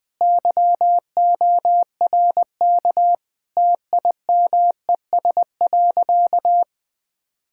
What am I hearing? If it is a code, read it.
Morse code, heard as a keyed tone, about 20 wpm, 700 Hz. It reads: YORK TIMES.
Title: Sentences with 3-Word Phrases - 20wpm